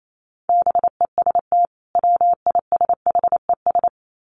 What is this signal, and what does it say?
Morse code, heard as a keyed tone, about 28 wpm, 700 Hz. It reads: BEST WISHES
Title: Common QSO Phrases 28wpm